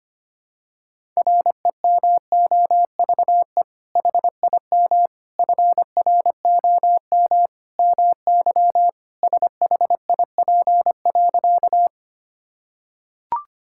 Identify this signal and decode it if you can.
Morse code, heard as a keyed tone, about 25 wpm, 700 Hz. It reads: REMOVE HIM FROM MY SHIP.
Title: Sentences from Top 1000 Words - Review Part 5 - 25wpm